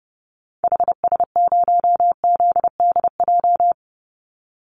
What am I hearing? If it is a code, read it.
Morse code, heard as a keyed tone, about 30 wpm, 700 Hz. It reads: HS0ZDJ
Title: Call Signs - 30wpm